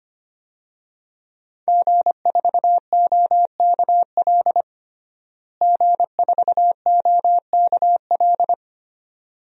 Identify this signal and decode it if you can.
Morse code, heard as a keyed tone, about 25 wpm, 700 Hz. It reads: G4OKL G4OKL
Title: DXCC-100 Call Signs - 5 Characters - Repeated 3x - 25wpm